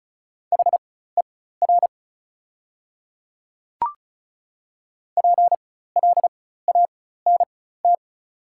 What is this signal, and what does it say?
Morse code, heard as a keyed tone, about 34 wpm, spaced as for 16 wpm, 690 Hz. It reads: HER PLANT
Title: 4x Character Spacing - Top 500 Words - 35wpm